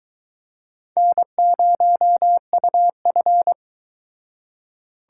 Morse code, heard as a keyed tone, about 22 wpm, 700 Hz. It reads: N0UF